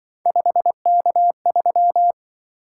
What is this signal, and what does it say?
Morse code, heard as a keyed tone, about 23 wpm, 700 Hz. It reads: 5K3
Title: Speed-Racing - Three Letter-Number Combinations - 20wpm